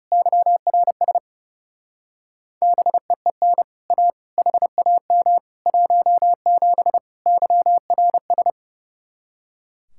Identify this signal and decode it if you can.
Morse code, heard as a keyed tone, about 36 wpm, 700 Hz. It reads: YRS BEEN A HAM 17 YRS
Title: Speed-Racing Common QSO Phrases - 30wpm